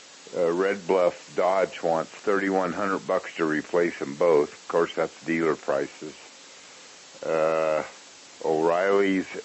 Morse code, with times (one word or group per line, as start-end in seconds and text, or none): none